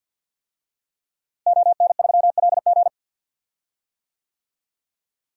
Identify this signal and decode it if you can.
Morse code, heard as a keyed tone, about 52 wpm, 700 Hz. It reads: ON3LG
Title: DXCC-100 Call Signs - 5 Characters - Repeated 2x - 50wpm